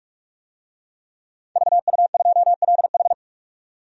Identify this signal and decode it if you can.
Morse code, heard as a keyed tone, about 43 wpm, 700 Hz. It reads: VU2LH
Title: DXCC-100 Call Signs - 5 Characters - 45wpm